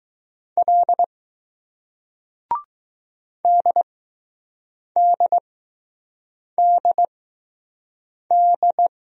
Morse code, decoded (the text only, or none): L DDDD